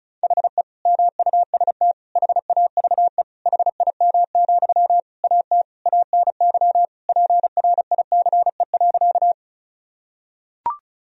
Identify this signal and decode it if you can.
Morse code, heard as a keyed tone, about 36 wpm, 700 Hz. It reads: HE MUST HAVE HIM, AT ANY PRICE.
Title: Sentences from Top 1000 Words - Review Part 4 - 35wpm